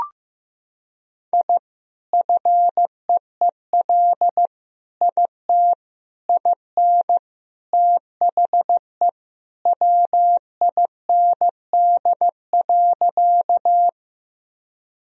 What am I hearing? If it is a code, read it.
Morse code, heard as a keyed tone, about 16 wpm, 700 Hz. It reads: I FEEL IT IN THE WIND.
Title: Sentences from Top 1000 Words - Review Part 2 - 15wpm